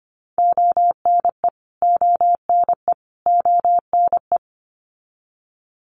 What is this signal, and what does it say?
Morse code, heard as a keyed tone, about 25 wpm, 700 Hz. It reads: ONE ONE ONE
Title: Top 500 Words Repeated 25wpm